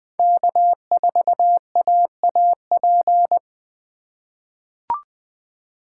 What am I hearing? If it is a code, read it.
Morse code, heard as a keyed tone, about 20 wpm, 700 Hz. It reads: K4AAP
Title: USA Call Signs - 5 Characters - 2x Character Spacing - 20wpm